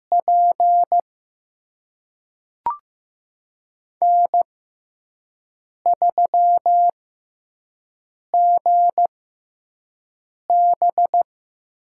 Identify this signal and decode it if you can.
Morse code, heard as a keyed tone, about 15 wpm, spaced as for 5 wpm, 700 Hz. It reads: P N3GB
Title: USA Call Signs - 6x Character Spacing - 15wpm